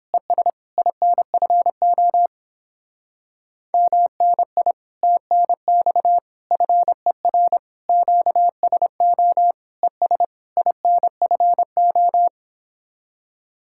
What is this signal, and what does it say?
Morse code, heard as a keyed tone, about 26 wpm, 700 Hz. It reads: ES INFO MNI TNX FER QSO ES INFO
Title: Speed-Racing Common QSO Phrases - 22wpm